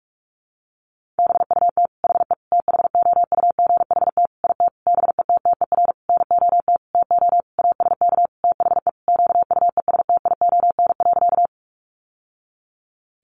Morse code, heard as a keyed tone, about 44 wpm, 700 Hz. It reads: BUT HE THOUGHT IT BETTER NOT TO ASK THE QUESTION.
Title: Sentences from Top 600 Words 45wpm